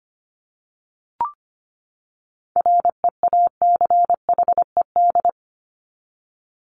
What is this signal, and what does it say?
Morse code, heard as a keyed tone, about 25 wpm, 700 Hz. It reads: REACHED
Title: ED Suffix - Top 100 Words with Base Words 25wpm